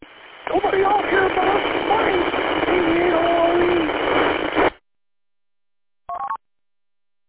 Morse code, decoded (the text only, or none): none